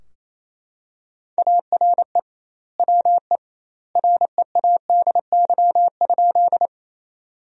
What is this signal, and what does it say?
Morse code, heard as a keyed tone, about 28 wpm, 700 Hz. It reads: ARE WE READY?
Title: 2x Word-Spacing - Sentences from Top 400 Words - 28wpm